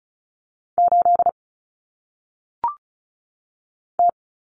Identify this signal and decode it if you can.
Morse code, heard as a keyed tone, about 36 wpm, 700 Hz. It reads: 8 T
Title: Common Cut Numbers - 1-Digit 35wpm